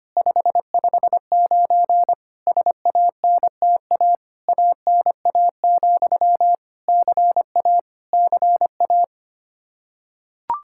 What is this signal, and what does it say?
Morse code, heard as a keyed tone, about 25 wpm, 700 Hz. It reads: 559 SANTA ANA, CA CA
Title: Practice QSO - Exchange 1 - Beginning - 25wpm